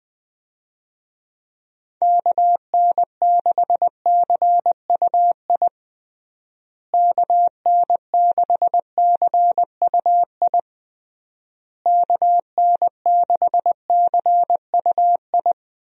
Morse code, heard as a keyed tone, about 20 wpm, 700 Hz. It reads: KN6CUI KN6CUI KN6CUI
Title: USA Call Signs - 6 Characters - Repeated 3x - 20wpm